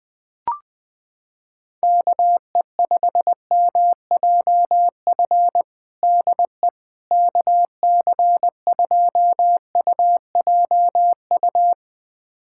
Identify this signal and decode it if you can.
Morse code, heard as a keyed tone, about 20 wpm, 700 Hz. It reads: KE5MJF DE KC2UJU
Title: USA Callsign DE Callsign - 6 Characters - 20wpm